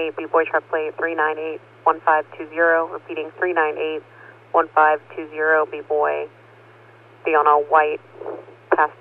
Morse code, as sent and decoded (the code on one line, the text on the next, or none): none